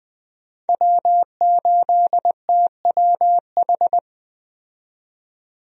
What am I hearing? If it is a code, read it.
Morse code, heard as a keyed tone, about 20 wpm, 700 Hz. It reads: W8TWH
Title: USA Call Signs - 5 Characters - 20wpm